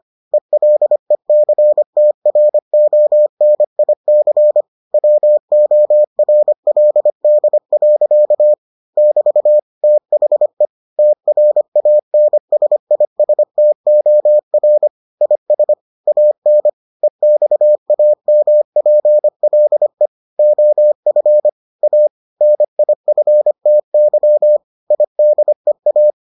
Morse code, heard as a keyed tone, about 25 wpm, 600 Hz. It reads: ELECTRONIC WORLD. <BT> THE TRANSISTOR IS AN EXAMPLE OF A NIFTY IDEA